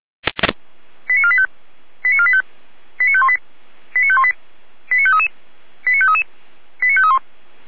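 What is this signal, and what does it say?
Five-tone selective call (ZVEI), heard as quick runs of five tones, tones of about 70 ms. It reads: 89485 89485 89518 89518 89533 89533 89531
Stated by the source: ZVEI five-tone selective call 89463, received by SDR.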